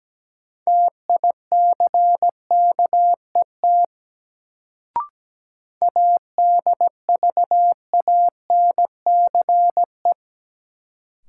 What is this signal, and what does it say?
Morse code, heard as a keyed tone, about 17 wpm, 700 Hz. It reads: TICKET ADVANCE
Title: Sentences from Top 1400 Words - New Word Drill - 17wpm